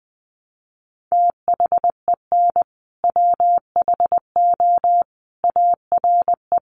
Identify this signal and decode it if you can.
Morse code, heard as a keyed tone, about 20 wpm, 700 Hz. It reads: THEN WHO ARE
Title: Sentences from Top 100 Words 20wpm